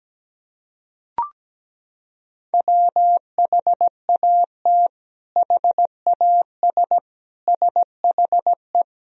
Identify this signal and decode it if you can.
Morse code, heard as a keyed tone, about 17 wpm, 700 Hz. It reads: WHAT HAS SHE
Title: Sentences from Top 100 Words and Abbreviations 17wpm